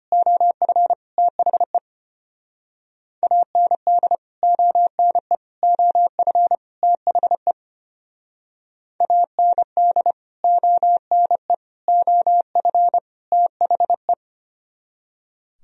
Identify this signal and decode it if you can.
Morse code, heard as a keyed tone, about 32 wpm, 700 Hz. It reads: OF THE AND ONE OF THE AND ONE OF THE